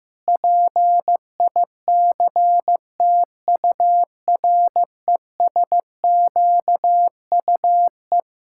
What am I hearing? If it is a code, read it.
Morse code, heard as a keyed tone, about 15 wpm, 700 Hz. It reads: PICTURESQUE
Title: Pangram Challenge v2 - 15wpm